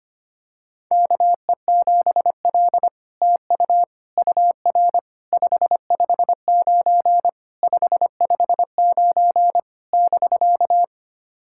Morse code, heard as a keyed tone, about 25 wpm, 700 Hz. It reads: KE7L TU UR 559 559 <BK>
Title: SOTA QSO Practice 25wpm